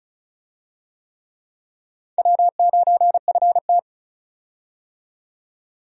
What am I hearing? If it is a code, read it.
Morse code, heard as a keyed tone, about 36 wpm, 700 Hz. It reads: W9FT
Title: USA Call Signs - 4 Characters - 35wpm